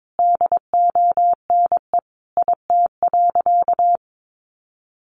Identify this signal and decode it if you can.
Morse code, heard as a keyed tone, about 22 wpm, 700 Hz. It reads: DONE IT.
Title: Sentences from Top 1200 Words 22wpm